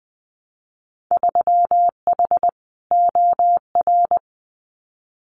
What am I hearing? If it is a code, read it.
Morse code, heard as a keyed tone, about 20 wpm, 700 Hz. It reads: 3H OR
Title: Winter Field Day Exchanges Encore 20wpm